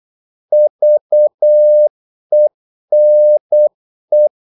Morse code, heard as a keyed tone, about 8 wpm, 600 Hz. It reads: VENE
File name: Korte ord DK 8wpm